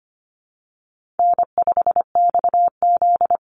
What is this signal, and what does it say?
Morse code, heard as a keyed tone, about 25 wpm, 700 Hz. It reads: N5XZ
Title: Easy Call Signs - 25wpm